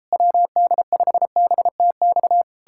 Morse code, heard as a keyed tone, about 37 wpm, 700 Hz. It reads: WD5BTX